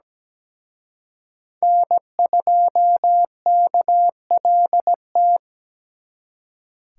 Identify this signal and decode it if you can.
Morse code, heard as a keyed tone, about 20 wpm, 700 Hz. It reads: N2KLT
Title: Speed Racing - USA Call Signs 17wpm